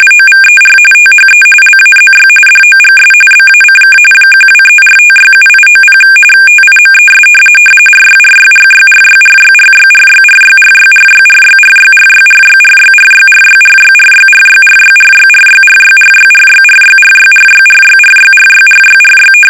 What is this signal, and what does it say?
RTTY, about 45.45 baud. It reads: OVER THE LAZY DOGS BACK 1234567890 TIMES RYRYRYRYRYRYRYRYRYRYRYRYRYRYRYRYRYRYRYRYRYRYRYRYRYRYRYRYRYRYRYRYRYRY
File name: rtty / RTTY Test 850